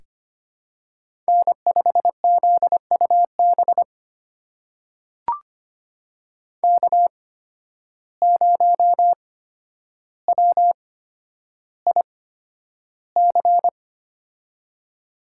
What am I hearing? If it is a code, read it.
Morse code, heard as a keyed tone, about 25 wpm, spaced as for 11 wpm, 700 Hz. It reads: N5ZUB K0WIC
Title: USA Call Signs - 5 Characters - 8x Character Spacing - 25wpm